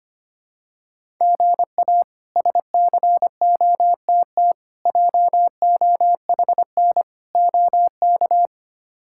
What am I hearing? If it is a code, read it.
Morse code, heard as a keyed tone, about 25 wpm, 700 Hz. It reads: GA SCOTT JOHN OK
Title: K1USN SST QSO Practice - 25wpm